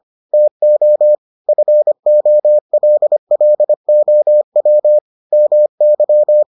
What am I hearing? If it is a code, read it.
Morse code, heard as a keyed tone, about 25 wpm, 600 Hz. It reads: TO FOLLOW MY